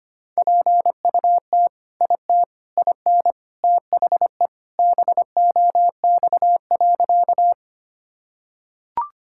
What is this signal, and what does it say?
Morse code, heard as a keyed tone, about 25 wpm, 700 Hz. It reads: PUT IT IN THE BOX.
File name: practice / 25wpm / Sentences with 2-Word Phrases - 25wpm